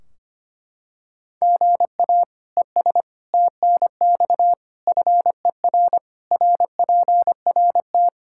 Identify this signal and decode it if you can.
Morse code, heard as a keyed tone, about 25 wpm, 700 Hz. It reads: GA ES TNX FER RPRT
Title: Practice QSO - Exchange 1 - Beginning - 25wpm